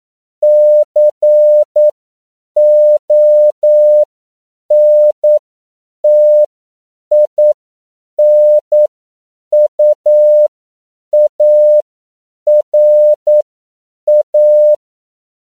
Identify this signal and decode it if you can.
Morse code, heard as a keyed tone, about 9 wpm, spaced as for 7 wpm, 600 Hz. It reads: CONTINUARA